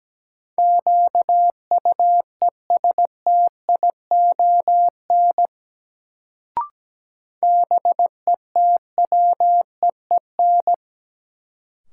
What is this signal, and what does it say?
Morse code, heard as a keyed tone, about 17 wpm, 700 Hz. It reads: QUESTION BETWEEN
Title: Sentences from Top 400 Words - New Word Drill - 17wpm